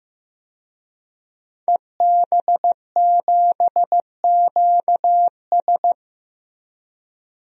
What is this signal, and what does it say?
Morse code, heard as a keyed tone, about 15 wpm, 700 Hz. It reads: EB7QS
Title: DXCC-100 Call Signs - 5 Characters - Repeated 3x - 15wpm